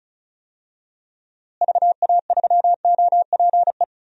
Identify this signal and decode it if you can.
Morse code, heard as a keyed tone, about 34 wpm, 700 Hz. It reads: VA3OPE
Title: DXCC-100 Call Signs - 6 Characters - Repeated 2x - 35wpm